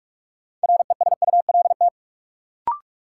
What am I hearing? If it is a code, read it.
Morse code, heard as a keyed tone, about 42 wpm, 700 Hz. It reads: RESULT
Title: Sets of 3 Words - New Word Drill - 45wpm